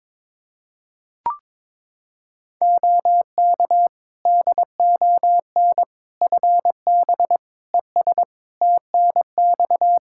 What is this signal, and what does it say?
Morse code, heard as a keyed tone, about 22 wpm, 700 Hz. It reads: OK DON FB ES TNX
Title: Practice QSO - Exchange 2 - Info - 22wpm